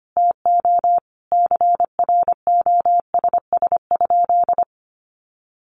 Morse code, heard as a keyed tone, about 25 wpm, 700 Hz. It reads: TO CROSS?
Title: Sentences from Top 300 Words 25wpm